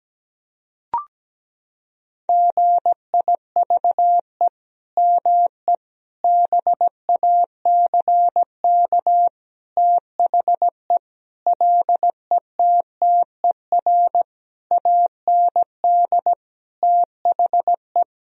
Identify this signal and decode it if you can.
Morse code, heard as a keyed tone, about 17 wpm, 700 Hz. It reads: GIVE ME BACK THE LETTER AND THE
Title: Sentences from Top 1000 Words - Review Part 4 - 17wpm